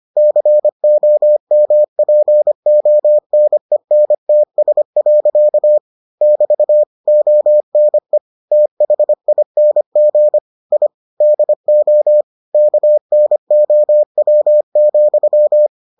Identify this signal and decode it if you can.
Morse code, heard as a keyed tone, about 25 wpm, 600 Hz. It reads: COMPONENTS. <BT> ONE THING I DO KNOW,